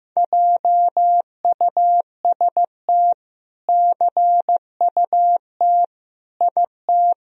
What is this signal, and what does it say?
Morse code, heard as a keyed tone, about 15 wpm, 700 Hz. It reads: JUST CUT IT
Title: Sentences from Top 1000 Words - Review Part 1 - 15wpm